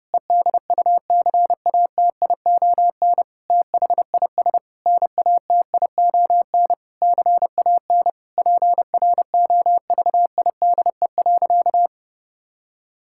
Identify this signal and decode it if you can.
Morse code, heard as a keyed tone, about 32 wpm, 700 Hz. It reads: EDUCATION THIS NATION CAN PROVIDE.
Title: Sentences from Top 1000 Words - Review Part 6 - 30wpm